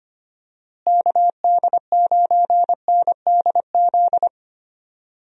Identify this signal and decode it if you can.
Morse code, heard as a keyed tone, about 25 wpm, 700 Hz. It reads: KD9NDZ
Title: USA Call Signs - 6 Characters - Repeated 3x - 25wpm